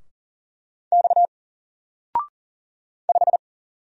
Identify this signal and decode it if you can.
Morse code, heard as a keyed tone, about 40 wpm, 700 Hz. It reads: X 5
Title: Single Character (Letter:Number:Punctuation:Prosign) 40wpm